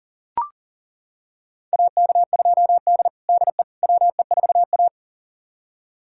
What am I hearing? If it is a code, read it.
Morse code, heard as a keyed tone, about 40 wpm, 700 Hz. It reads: AK2D DE WE4A
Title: USA Callsign DE Callsign - 4 Characters - 40wpm